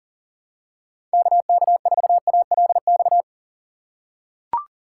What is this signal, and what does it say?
Morse code, heard as a keyed tone, about 40 wpm, 700 Hz. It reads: KK4ALX
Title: USA Call Signs - 6 Characters - 4x Character Spacing - 40wpm